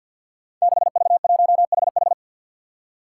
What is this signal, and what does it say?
Morse code, heard as a keyed tone, about 47 wpm, 700 Hz. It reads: 6V1HH
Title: Five Challenging Letter-Number Combinations 50wpm